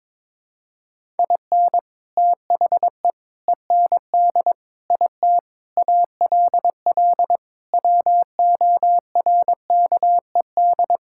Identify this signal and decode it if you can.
Morse code, heard as a keyed tone, about 22 wpm, 700 Hz. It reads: IN THE END IT ALL WORKED
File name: practice / 22wpm / Sentences with 3-Word Phrases - 22wpm